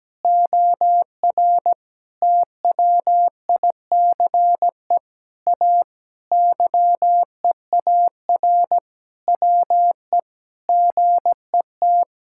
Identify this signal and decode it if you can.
Morse code, heard as a keyed tone, about 17 wpm, 700 Hz. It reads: OR TWICE A YEAR WE GET
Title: Sentences from Top 1600 Words 17wpm